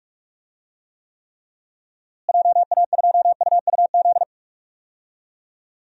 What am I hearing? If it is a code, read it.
Morse code, heard as a keyed tone, about 44 wpm, 700 Hz. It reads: JA2UUZ
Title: DXCC-100 Call Signs - 6 Characters - Repeated 3x - 45wpm